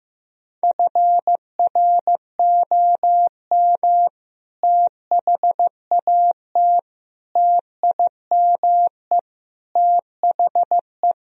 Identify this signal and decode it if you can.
Morse code, heard as a keyed tone, about 16 wpm, 700 Hz. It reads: FROM THAT TIME THE
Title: Sentences from Top 1000 Words - Review Part 1 - 15wpm